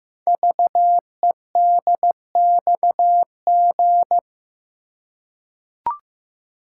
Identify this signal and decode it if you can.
Morse code, heard as a keyed tone, about 15 wpm, 700 Hz. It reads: VEDXG
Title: Code Groups - 5 Letters - 15wpm